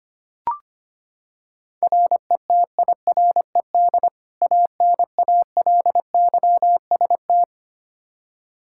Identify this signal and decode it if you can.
Morse code, heard as a keyed tone, about 25 wpm, 700 Hz. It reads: RETIRED ANALYST
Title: Practice QSO - Exchange 3 - More Info - 25wpm